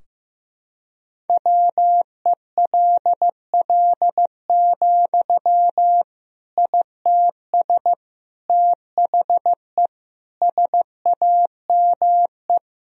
Morse code, heard as a keyed tone, about 14 wpm, 700 Hz. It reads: WELL, ITS THE SAME
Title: Sentences from Top 1000 Words - Review Part 4 - 15wpm